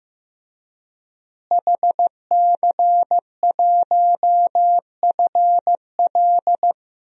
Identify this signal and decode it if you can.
Morse code, heard as a keyed tone, about 16 wpm, 700 Hz. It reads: HC1FL
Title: DXCC-100 Call Signs - 5 Characters - Repeated 2x - 15wpm